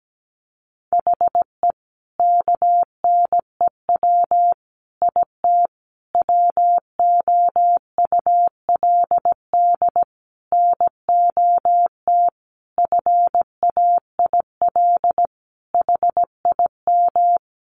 Morse code, heard as a keyed tone, about 17 wpm, 700 Hz. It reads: HE KNEW IT WOULD NOT FAIL HIM
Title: Sentences from Top 1100 Words 17wpm